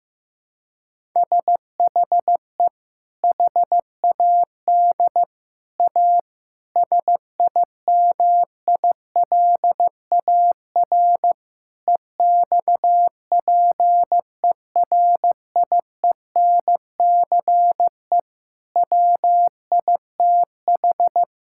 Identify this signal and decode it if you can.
Morse code, heard as a keyed tone, about 16 wpm, 700 Hz. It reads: SHE HAD A SIMILAR EXPERIENCE WITH
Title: Sentences from Top 1000 Words - Review Part 5 - 15wpm